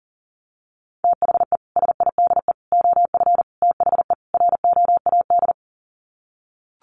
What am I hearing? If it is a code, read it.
Morse code, heard as a keyed tone, about 40 wpm, 700 Hz. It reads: THE SIDE OF THE ROAD
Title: Sets of 5 Words 40wpm